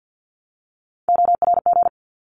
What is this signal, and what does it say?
Morse code, heard as a keyed tone, about 50 wpm, 700 Hz. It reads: ORG
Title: ARRL Sections 50wpm